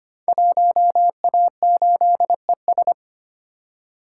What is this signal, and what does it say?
Morse code, heard as a keyed tone, about 25 wpm, 700 Hz. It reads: 1A8ES
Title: Code Groups - 5 Letter:Numbers - 25wpm